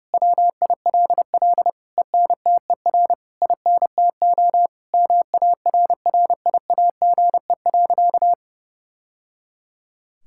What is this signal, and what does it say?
Morse code, heard as a keyed tone, about 30 wpm, 700 Hz. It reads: WILL ENTER INTO MARRIAGE.
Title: Sentences from Top 1000 Words - Review Part 6 - 30wpm